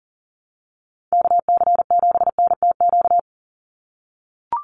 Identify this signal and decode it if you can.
Morse code, heard as a keyed tone, about 40 wpm, 700 Hz. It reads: KC7NTQ
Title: USA Call Signs 40wpm